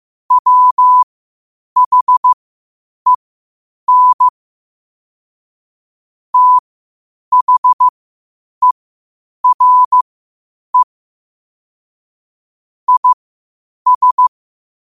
Morse code, heard as a keyed tone, about 15 wpm, spaced as for 8 wpm, 1000 Hz. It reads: WHEN THERE IS
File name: qotdfiles / 8wpm